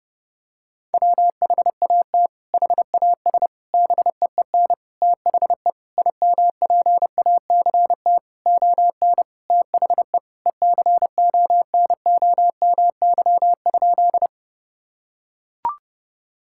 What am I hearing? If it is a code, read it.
Morse code, heard as a keyed tone, about 30 wpm, 700 Hz. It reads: WHAT HAS BEEN THE IMPACT ON THE ECONOMY?
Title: Sentences from Top 1000 Words - Review Part 6 - 30wpm